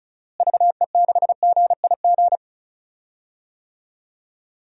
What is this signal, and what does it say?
Morse code, heard as a keyed tone, about 34 wpm, 700 Hz. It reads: VE6GIG
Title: DXCC-100 Call Signs - 6 Characters - Repeated 3x - 35wpm